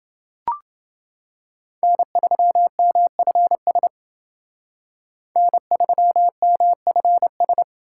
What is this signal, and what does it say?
Morse code, heard as a keyed tone, about 29 wpm, 700 Hz. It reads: N3MFS N3MFS